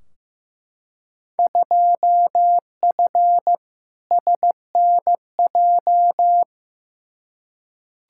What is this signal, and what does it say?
Morse code, heard as a keyed tone, about 15 wpm, 700 Hz. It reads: 2F SNJ
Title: Field Day Exchanges Encore - 15wpm